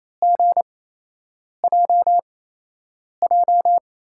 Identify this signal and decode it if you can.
Morse code, heard as a keyed tone, about 28 wpm, spaced as for 8 wpm, 700 Hz. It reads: GJJ
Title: USA Call Signs - 6 Characters - 8x Character Spacing - 28wpm